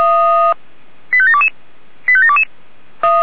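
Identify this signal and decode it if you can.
Five-tone selective call (ZVEI), heard as quick runs of five tones, tones of about 70 ms. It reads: 85722 85722
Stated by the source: ZVEI five-tone selective call received on an SDR.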